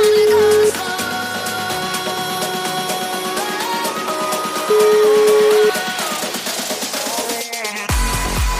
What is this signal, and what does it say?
Ringback tone (Europe, 425 Hz), with noise.